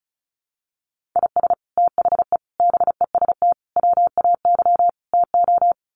Morse code, heard as a keyed tone, about 34 wpm, 700 Hz. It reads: IS THE BEST WAY TO
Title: Sets of 5 Words Encore 35wpm